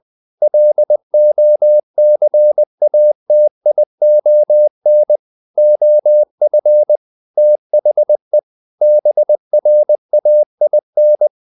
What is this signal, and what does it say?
Morse code, heard as a keyed tone, about 20 wpm, 600 Hz. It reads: LOCATION OF THE BRAIN
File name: the old man and the sea part 2 20wpm